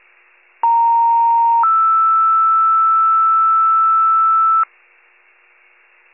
Two-tone page, heard as a one-second tone, then a three-second tone, about 930 then 1320 Hz. It is heard from 0.6 to 4.6 s.
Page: County Wide Paging Archive